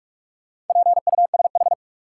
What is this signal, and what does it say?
Morse code, heard as a keyed tone, about 42 wpm, 700 Hz. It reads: PUSH
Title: Sentences from Top 1300 Words - New Word Drill Alt - 45wpm